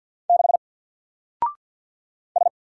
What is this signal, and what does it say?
Morse code, heard as a keyed tone, about 48 wpm, 700 Hz. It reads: 6 S
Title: Single Letter-Number Alt 50wpm